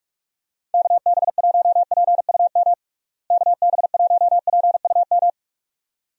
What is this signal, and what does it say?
Morse code, heard as a keyed tone, about 42 wpm, 700 Hz. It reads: KB1PUM KB1PUM
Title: USA Call Signs - 6 Characters - Repeated 2x - 45wpm